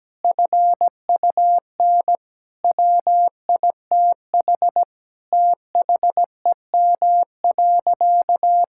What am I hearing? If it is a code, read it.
Morse code, heard as a keyed tone, about 17 wpm, 700 Hz. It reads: FUN WITH THEM.
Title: Sentences from Top 1000 Words - Review Part 5 - 17wpm